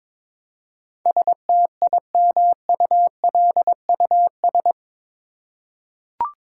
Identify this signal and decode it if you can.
Morse code, heard as a keyed tone, about 22 wpm, 700 Hz. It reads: STIMULUS
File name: practice / 22wpm / Eight Letter Words - 22wpm